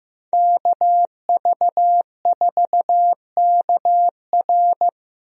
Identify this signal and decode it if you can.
Morse code, heard as a keyed tone, about 15 wpm, 700 Hz. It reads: KV4KR
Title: USA Call Signs - 5 Characters - Repeated 3x - 15wpm